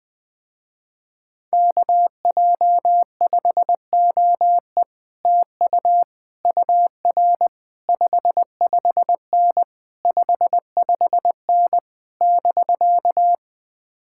Morse code, heard as a keyed tone, about 20 wpm, 700 Hz. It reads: KJ5OE TU UR 55N 55N <BK>
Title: Speed-Racing Common QSO Phrases - 20wpm